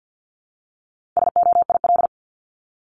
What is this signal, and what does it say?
Morse code, heard as a keyed tone, about 50 wpm, 700 Hz. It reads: SOIL
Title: Top 501-750 Words 50wpm